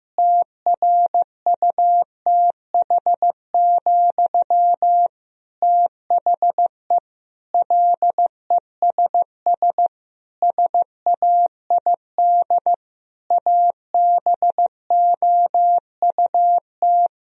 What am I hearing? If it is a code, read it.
Morse code, heard as a keyed tone, about 16 wpm, 700 Hz. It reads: TRUTH, THE LESS SAID ABOUT
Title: Sentences from Top 1000 Words - Review Part 2 - 15wpm